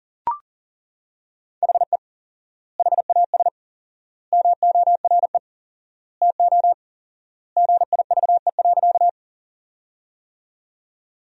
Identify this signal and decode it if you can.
Morse code, heard as a keyed tone, about 37 wpm, 700 Hz. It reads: HE HAS MORE TO GIVE.
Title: 4x Word-Spacing - Sentences from Top 200 Words - 40wpm